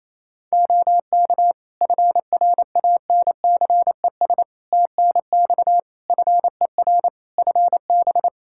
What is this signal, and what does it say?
Morse code, heard as a keyed tone, about 28 wpm, 700 Hz. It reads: OK FRANCES TNX FER FB
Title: Practice QSO - Exchange 4 - Ending - 28wpm